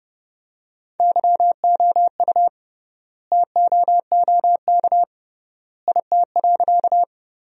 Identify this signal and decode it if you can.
Morse code, heard as a keyed tone, about 30 wpm, 700 Hz. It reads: YOU TOOK IT.
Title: 3x Word-Spacing - Sentences from Top 300 Words - 30wpm